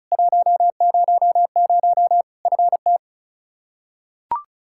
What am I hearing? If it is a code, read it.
Morse code, heard as a keyed tone, about 36 wpm, 700 Hz. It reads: 100 FT
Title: Practice QSO - Exchange 2 - Info - 35wpm